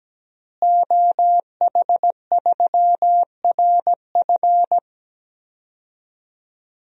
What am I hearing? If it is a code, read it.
Morse code, heard as a keyed tone, about 17 wpm, 700 Hz. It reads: OH3RF
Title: DXCC-100 Call Signs - 5 Characters - Repeated 2x - 17wpm